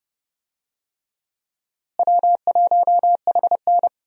Morse code, heard as a keyed tone, about 30 wpm, 700 Hz. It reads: W1HN
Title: USA Call Signs - 4 Characters - Repeated 3x - 30wpm